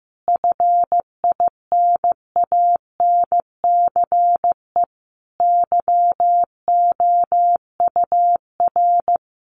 Morse code, heard as a keyed tone, about 15 wpm, 700 Hz. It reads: FINANCE YOUR
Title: Sentences from Top 1500 Words 15wpm